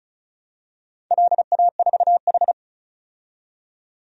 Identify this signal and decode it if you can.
Morse code, heard as a keyed tone, about 34 wpm, 700 Hz. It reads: LA4H
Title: DXCC-100 Call Signs - 4 Characters - 35wpm